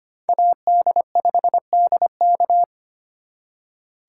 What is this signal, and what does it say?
Morse code, heard as a keyed tone, about 32 wpm, 700 Hz. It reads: AD5DK